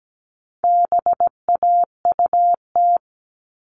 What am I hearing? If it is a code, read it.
Morse code, heard as a keyed tone, about 17 wpm, 700 Hz. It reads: BAUT
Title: All Cut Numbers - 4-Digit 17wpm